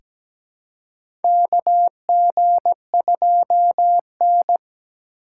Morse code, heard as a keyed tone, about 17 wpm, 700 Hz. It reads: KG2N
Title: USA Call Signs - 3x Character Spacing - 17wpm